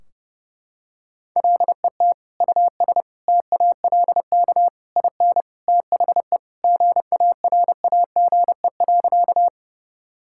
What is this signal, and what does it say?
Morse code, heard as a keyed tone, about 30 wpm, 700 Hz. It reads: LET US TALK IN THE GARAGE.
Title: Sentences from Top 1600 Words 30wpm